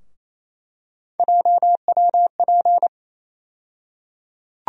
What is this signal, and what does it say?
Morse code, heard as a keyed tone, about 28 wpm, 700 Hz. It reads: JWP
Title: USA Call Sign Postfixes - 3 Letters 28wpm